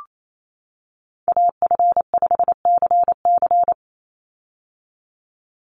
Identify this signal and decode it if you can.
Morse code, heard as a keyed tone, about 28 wpm, 700 Hz. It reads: AF5CC
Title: Call Signs - 28wpm